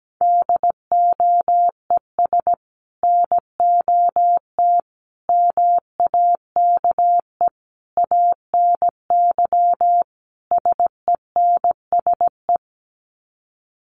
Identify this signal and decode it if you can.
Morse code, heard as a keyed tone, about 17 wpm, 700 Hz. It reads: DOES NOT MAKE ANY SENSE
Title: Sets of 5 Words Encore2 17wpm